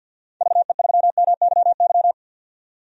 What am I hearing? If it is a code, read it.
Morse code, heard as a keyed tone, about 51 wpm, 700 Hz. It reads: VE3MYY
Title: DXCC-100 Call Signs - 6 Characters - 50wpm